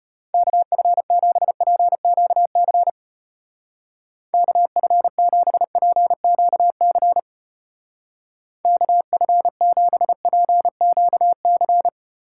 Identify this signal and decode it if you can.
Morse code, heard as a keyed tone, about 29 wpm, 700 Hz. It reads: KF7PQC KF7PQC KF7PQC
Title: Speed Racing - USA 2x3 Call Signs 25wpm